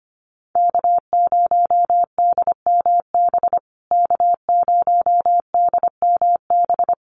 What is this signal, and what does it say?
Morse code, heard as a keyed tone, about 25 wpm, 700 Hz. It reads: K0DMB K0DMB
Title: USA Call Signs Repeated 2x 25wpm